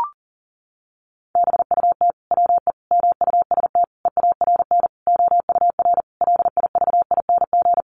Morse code, heard as a keyed tone, about 40 wpm, 700 Hz. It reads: BUT WE MUST EARN OUR LIVING
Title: Sentences from Top 1500 Words 40wpm